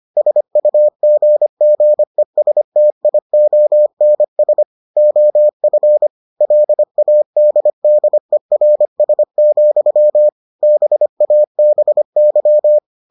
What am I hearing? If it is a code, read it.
Morse code, heard as a keyed tone, about 25 wpm, 600 Hz. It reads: SUGGESTIONS OF LADDERS, BABY